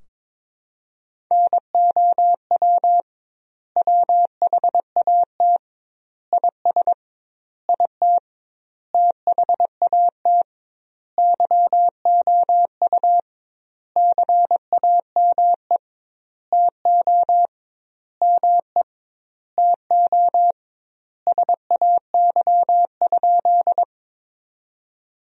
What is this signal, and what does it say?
Morse code, heard as a keyed tone, about 21 wpm, 700 Hz. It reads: NOW WHAT IS IT THAT YOU CAME TO ME TO SAY?
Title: 2x Word-Spacing - Sentences from Top 200 Words - 22wpm